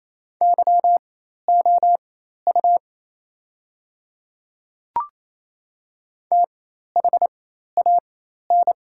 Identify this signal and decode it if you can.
Morse code, heard as a keyed tone, about 28 wpm, spaced as for 13 wpm, 700 Hz. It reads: YOU THAN
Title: 4x Character Spacing - Top 100 Words - 28wpm